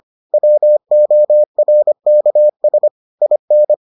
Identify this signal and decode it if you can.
Morse code, heard as a keyed tone, about 25 wpm, 600 Hz. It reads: WORKS IN